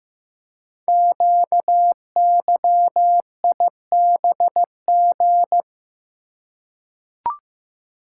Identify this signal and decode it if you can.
Morse code, heard as a keyed tone, about 15 wpm, 700 Hz. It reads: QYIBG
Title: Code Groups - 5 Letters - 15wpm